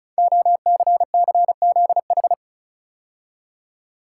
Morse code, heard as a keyed tone, about 36 wpm, 690 Hz. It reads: OCCZH
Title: Code Groups - 5 Letters - 35wpm